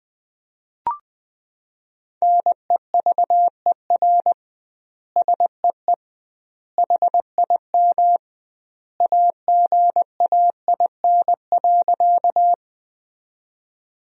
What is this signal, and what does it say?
Morse code, heard as a keyed tone, about 19 wpm, 700 Hz. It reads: NEVER SEE HIM AGAIN.
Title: 2x Word-Spacing - Sentences from Top 300 Words - 20wpm